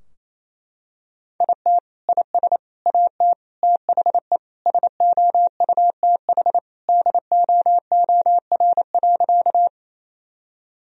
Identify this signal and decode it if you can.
Morse code, heard as a keyed tone, about 27 wpm, 700 Hz. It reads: IT IS AT THE SOUTH DOOR.
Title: Sentences from Top 1000 Words - Review Part 2 - 28wpm